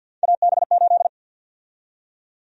Morse code, heard as a keyed tone, about 53 wpm, 700 Hz. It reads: AB8
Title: USA Call Sign Prefixes - 50wpm